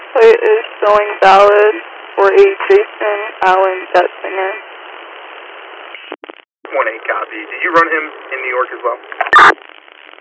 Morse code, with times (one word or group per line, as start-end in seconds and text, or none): none